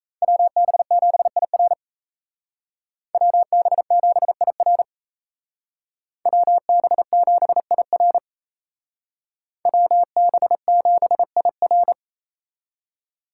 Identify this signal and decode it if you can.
Morse code, heard as a keyed tone, about 32 wpm, 700 Hz. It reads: WB7IR WB7IR WB7IR WB7IR
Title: Speed Racing - USA 2x2 Call Signs 28wpm